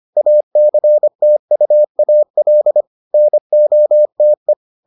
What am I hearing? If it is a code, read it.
Morse code, heard as a keyed tone, about 25 wpm, 600 Hz. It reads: ACTUAL NOTE